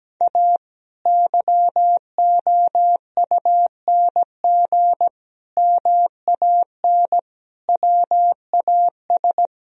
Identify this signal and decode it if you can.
Morse code, heard as a keyed tone, about 17 wpm, 700 Hz. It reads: A YOUNG MAN WAS
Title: Sentences from Top 1000 Words - Review Part 3 - 17wpm